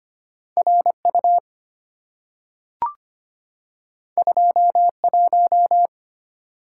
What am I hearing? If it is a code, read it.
Morse code, heard as a keyed tone, about 25 wpm, 690 Hz. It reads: RU 21
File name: Two Letter-Number Combinations - 25wpm